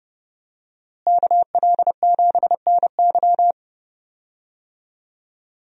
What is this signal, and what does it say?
Morse code, heard as a keyed tone, about 30 wpm, 700 Hz. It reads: KL7NY
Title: USA Call Signs - 5 Characters - Repeated 3x - 30wpm